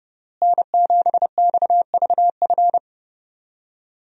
Morse code, heard as a keyed tone, about 30 wpm, 700 Hz. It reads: N7XVF
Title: USA Call Signs - 5 Characters - Repeated 2x - 30wpm